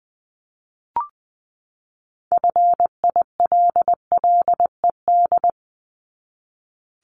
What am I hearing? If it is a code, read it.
Morse code, heard as a keyed tone, about 20 wpm, 700 Hz. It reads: FILLED
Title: ED Suffix - Top 100 Words with Base Words 20wpm